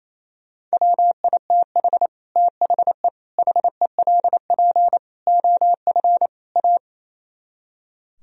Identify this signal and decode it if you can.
Morse code, heard as a keyed tone, about 35 wpm, 700 Hz. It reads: WITH THE HELP OF A
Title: Speed-Racing - Sets of 5 Words 28wpm